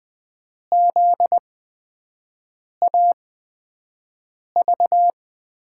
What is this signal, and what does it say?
Morse code, heard as a keyed tone, about 20 wpm, spaced as for 5 wpm, 700 Hz. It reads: ZAV
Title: USA Call Signs - 5 Characters - 8x Character Spacing - 20wpm